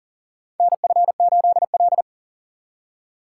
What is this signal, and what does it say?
Morse code, heard as a keyed tone, about 37 wpm, 700 Hz. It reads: NF8L
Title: USA Call Signs - 4 Characters - 2x Character Spacing - 40wpm